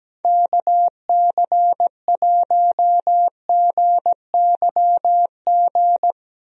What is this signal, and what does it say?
Morse code, heard as a keyed tone, about 17 wpm, 700 Hz. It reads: KC1GYG
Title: USA Call Signs - 6 Characters - Repeated 2x - 17wpm